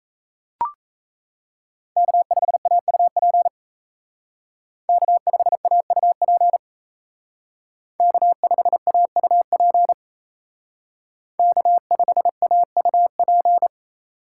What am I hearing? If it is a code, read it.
Morse code, heard as a keyed tone, about 34 wpm, 700 Hz. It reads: K5AUP K5AUP K5AUP K5AUP
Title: Speed Racing - USA 1x3 Call Signs 28wpm